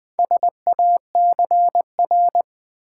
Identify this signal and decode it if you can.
Morse code, heard as a keyed tone, about 20 wpm, 700 Hz. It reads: SACR
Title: Build A Word - 2 Letters At A Time - 6-Letter Words - 20wpm